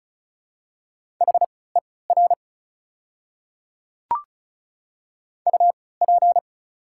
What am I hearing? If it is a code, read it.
Morse code, heard as a keyed tone, about 36 wpm, spaced as for 20 wpm, 690 Hz. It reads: HER UP
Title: 3x Character Spacing - Top 300 Words - 35wpm